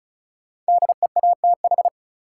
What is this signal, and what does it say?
Morse code, heard as a keyed tone, about 34 wpm, 700 Hz. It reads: DEATH
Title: Sentences from Top 1100 Words - New Word Drill Alt - 35wpm